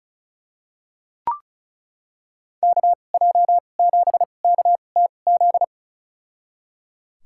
Morse code, heard as a keyed tone, about 34 wpm, spaced as for 25 wpm, 700 Hz. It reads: KJ7KTZ
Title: USA Call Signs - 6 Characters - 2x Character Spacing - 35wpm